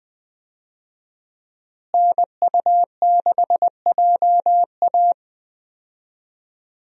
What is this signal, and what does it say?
Morse code, heard as a keyed tone, about 20 wpm, 700 Hz. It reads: NU6JA
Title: USA Call Signs - 5 Characters - Repeated 2x - 20wpm